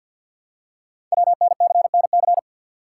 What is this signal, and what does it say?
Morse code, heard as a keyed tone, about 47 wpm, 700 Hz. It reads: WNCNC
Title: Code Groups - 5 Letters - 50wpm